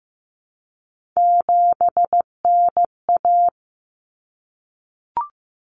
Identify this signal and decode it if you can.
Morse code, heard as a keyed tone, about 15 wpm, 700 Hz. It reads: 7NA
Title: Common Cut Numbers - 3-Digit 15wpm